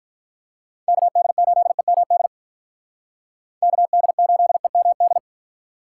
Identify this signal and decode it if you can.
Morse code, heard as a keyed tone, about 44 wpm, 700 Hz. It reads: KD8EMD KD8EMD
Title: Speed Racing - USA Call Signs 35wpm